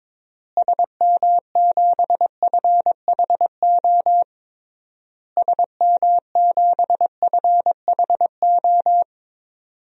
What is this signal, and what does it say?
Morse code, heard as a keyed tone, about 22 wpm, 700 Hz. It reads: SM7FHO SM7FHO
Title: DXCC-100 Call Signs - 6 Characters - Repeated 3x - 22wpm